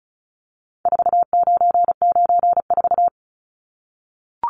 Morse code, heard as a keyed tone, about 34 wpm, 700 Hz. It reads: 4994
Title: Numbers - 4-Digit 35wpm